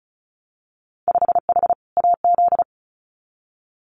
Morse code, heard as a keyed tone, about 36 wpm, 700 Hz. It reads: 5H AZ
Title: Winter Field Day Exchanges Encore 35wpm